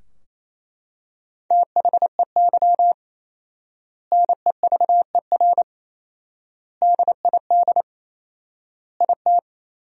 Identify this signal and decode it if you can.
Morse code, heard as a keyed tone, about 28 wpm, 700 Hz. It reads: THEY NEVER DID IT
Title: 4x Word-Spacing - Sentences from Top 300 Words - 28wpm